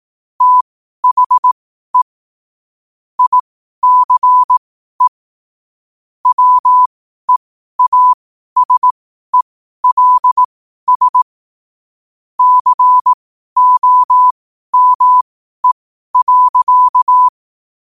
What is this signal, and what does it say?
Morse code, heard as a keyed tone, about 18 wpm, spaced as for 12 wpm, 1000 Hz. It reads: THE ICE WEASELS COME.
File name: qotdfiles / 12wpm